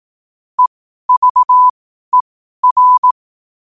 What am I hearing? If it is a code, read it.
Morse code, heard as a keyed tone, about 18 wpm, spaced as for 12 wpm, 1000 Hz. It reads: EVER
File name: qotdfiles / 12wpm